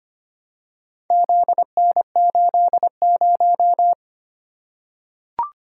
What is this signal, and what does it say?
Morse code, heard as a keyed tone, about 25 wpm, 700 Hz. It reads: ZN80
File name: practice / 25wpm / Four Letter-Number Combinations - 25wpm